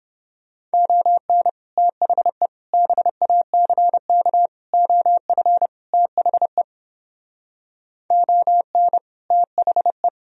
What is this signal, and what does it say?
Morse code, heard as a keyed tone, about 29 wpm, 700 Hz. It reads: ON THE BACK OF THE ON THE